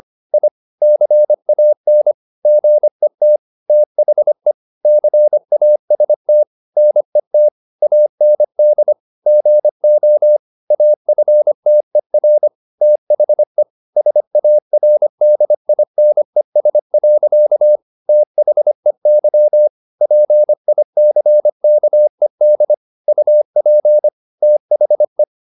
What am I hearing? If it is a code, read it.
Morse code, heard as a keyed tone, about 25 wpm, 600 Hz. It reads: I CAN GET THE CAST NET AND GO AFTER THE SARDINES. THEY PICKED UP THE
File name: the old man and the sea part 1 25wpm